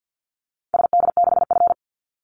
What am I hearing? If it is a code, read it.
Morse code, heard as a keyed tone, about 50 wpm, 700 Hz. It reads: SD6F
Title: Call Signs Repeated 3x - 50wpm